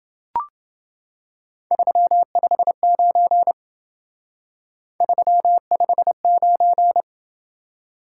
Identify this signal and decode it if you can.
Morse code, heard as a keyed tone, about 28 wpm, 700 Hz. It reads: 359 359